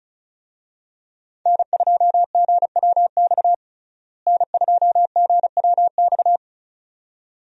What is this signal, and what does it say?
Morse code, heard as a keyed tone, about 36 wpm, 700 Hz. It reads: N2GWX N2GWX
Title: USA Call Signs - 5 Characters - Repeated 2x - 35wpm